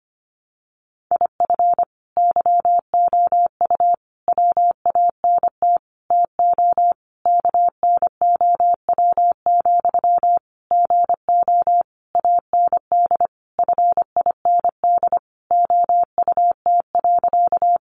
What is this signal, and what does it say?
Morse code, heard as a keyed tone, about 25 wpm, 700 Hz. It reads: IF YOU WANT TO KNOW, GO AND FIND OUT.
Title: Sentences from Top 200 Words 25wpm